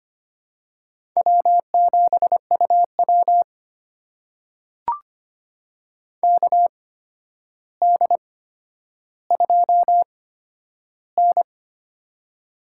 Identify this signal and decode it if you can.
Morse code, heard as a keyed tone, about 26 wpm, spaced as for 11 wpm, 700 Hz. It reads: W7UW KD2N
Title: USA Call Signs - 4 Characters - 8x Character Spacing - 25wpm